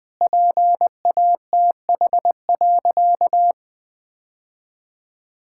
Morse code, heard as a keyed tone, about 20 wpm, 700 Hz. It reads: PATH.
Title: Sentences with 2-Word Phrases - Spoken Beginning - 20wpm